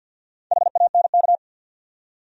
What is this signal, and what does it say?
Morse code, heard as a keyed tone, about 53 wpm, 700 Hz. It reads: HANK
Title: USA Common Names - Max 4 Letters - 50wpm